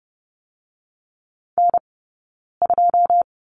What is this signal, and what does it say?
Morse code, heard as a keyed tone, about 30 wpm, 700 Hz. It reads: N 2
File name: Lesson 15d - Number 5 - Two Characters 30wpm